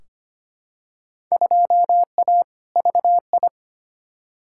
Morse code, heard as a keyed tone, about 25 wpm, 700 Hz. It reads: 2A VI
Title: Field Day Exchanges Encore - 25wpm